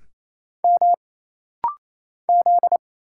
Morse code, heard as a keyed tone, about 27 wpm, 700 Hz. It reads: M Z
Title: Single Letter-Number - Rapid-Fire 28wpm